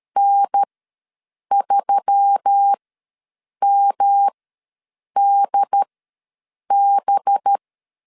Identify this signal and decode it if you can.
Morse code, heard as a keyed tone, about 13 wpm, spaced as for 7 wpm, 800 Hz. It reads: N3MDB
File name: LEZIONI / 6 PKX0 / 6 PKX ws 12wpm 3106